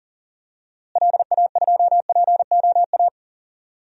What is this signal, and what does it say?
Morse code, heard as a keyed tone, about 40 wpm, 700 Hz. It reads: LA2POA
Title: DXCC-100 Call Signs - 6 Characters - Repeated 2x - 40wpm